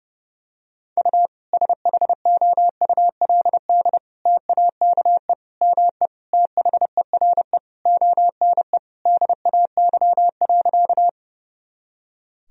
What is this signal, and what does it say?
Morse code, heard as a keyed tone, about 30 wpm, 700 Hz. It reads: U SHOULD TAKE ME THERE ONE DAY.
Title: Sentences from Top 300 Words and Abbreviations 30wpm